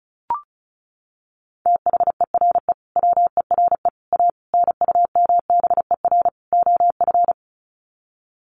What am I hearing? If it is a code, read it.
Morse code, heard as a keyed tone, about 34 wpm, 700 Hz. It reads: THERE WERE A NUMBER OF
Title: Sets of 5 Words Encore2 35wpm